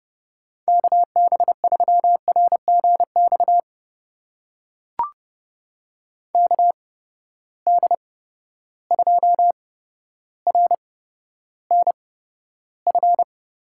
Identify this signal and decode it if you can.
Morse code, heard as a keyed tone, about 30 wpm, spaced as for 13 wpm, 700 Hz. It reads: KB3RGX KD2RNF
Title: USA Call Signs - 6 Characters - 8x Character Spacing - 30wpm